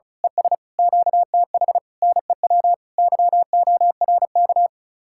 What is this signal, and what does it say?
Morse code, heard as a keyed tone, about 32 wpm, 700 Hz. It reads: ES QTH NEW YORK
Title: Speed-Racing Common QSO Phrases - 35wpm